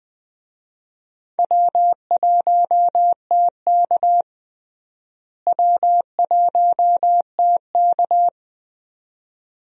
Morse code, heard as a keyed tone, about 20 wpm, 700 Hz. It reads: W1TK W1TK
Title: USA Call Signs - 4 Characters - Repeated 2x - 20wpm